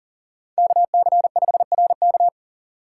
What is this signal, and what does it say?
Morse code, heard as a keyed tone, about 40 wpm, 700 Hz. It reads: KC5RK
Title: USA Call Signs - 5 Characters - 4x Character Spacing - 40wpm